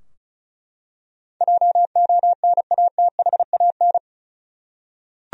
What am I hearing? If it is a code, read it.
Morse code, heard as a keyed tone, about 34 wpm, 700 Hz. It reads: JONATHAN
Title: USA Common Names - 35wpm